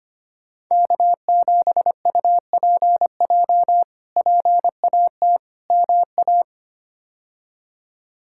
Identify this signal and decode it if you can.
Morse code, heard as a keyed tone, about 25 wpm, 700 Hz. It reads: K7UPJ PAT MA
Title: K1USN SST QSO Practice - 25wpm